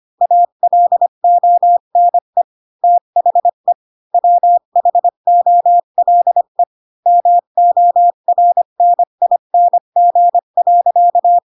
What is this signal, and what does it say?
Morse code, heard as a keyed tone, about 25 wpm, 700 Hz. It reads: ALONE THE WHOLE MORNING.
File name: Pinocchio / Pinocchio ch8 20 25 ik1hge